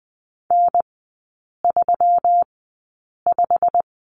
Morse code, heard as a keyed tone, about 20 wpm, spaced as for 8 wpm, 700 Hz. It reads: N35
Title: 159 Lesson 26e - Letter g - Three Characters 20wpm